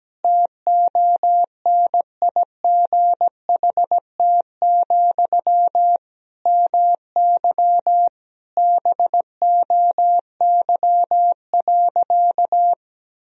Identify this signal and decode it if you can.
Morse code, heard as a keyed tone, about 17 wpm, 700 Hz. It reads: TONIGHT, MY BOY.
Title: Sentences from Top 1000 Words - Review Part 3 - 17wpm